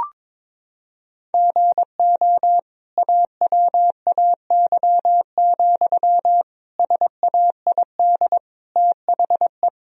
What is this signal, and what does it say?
Morse code, heard as a keyed tone, about 22 wpm, 700 Hz. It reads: GO AWAY, SAID THE
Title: Sentences from Top 1000 Words - Review Part 2 - 22wpm